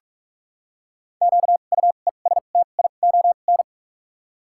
Morse code, heard as a keyed tone, about 46 wpm, spaced as for 33 wpm, 700 Hz. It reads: QUESTION
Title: 2x Character Spacing - Top 500 Words - 45wpm